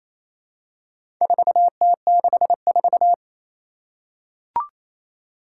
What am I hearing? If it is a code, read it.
Morse code, heard as a keyed tone, about 28 wpm, 700 Hz. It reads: 4T64
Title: Four Challenging Letter-Number Combinations 28wpm